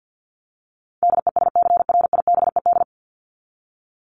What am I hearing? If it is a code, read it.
Morse code, heard as a keyed tone, about 50 wpm, 700 Hz. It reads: DESCRIBED
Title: ED Suffix - Top 100 Words with Base Words 50wpm